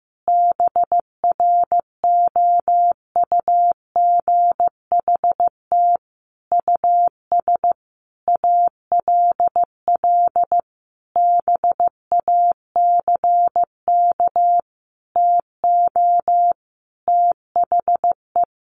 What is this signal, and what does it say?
Morse code, heard as a keyed tone, about 15 wpm, 700 Hz. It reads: BROUGHT US ALL BACK TO THE
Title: Sentences from Top 1500 Words 15wpm